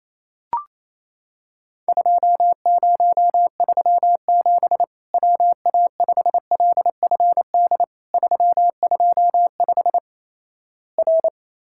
Morse code, heard as a keyed tone, about 28 wpm, 700 Hz. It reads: 2037 WA5LFD 325 R
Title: Worked All Europe DX Contest - QTC Exchange - 28wpm